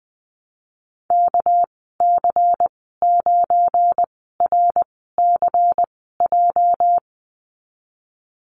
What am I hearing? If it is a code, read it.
Morse code, heard as a keyed tone, about 20 wpm, spaced as for 15 wpm, 700 Hz. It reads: KC9RCJ
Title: 219 Lesson 35g - Number 0 - Call Signs 20wpm